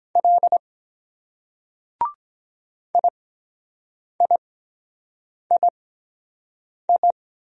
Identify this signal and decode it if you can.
Morse code, heard as a keyed tone, about 23 wpm, spaced as for 7 wpm, 700 Hz. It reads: L IIII